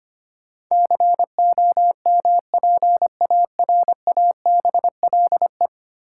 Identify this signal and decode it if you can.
Morse code, heard as a keyed tone, about 25 wpm, 700 Hz. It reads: COMPARABLE
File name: Ten Letter Words - 25wpm